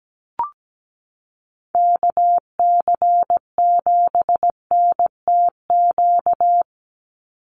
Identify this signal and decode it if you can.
Morse code, heard as a keyed tone, about 17 wpm, 700 Hz. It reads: KC7NTQ
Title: USA Call Signs 17wpm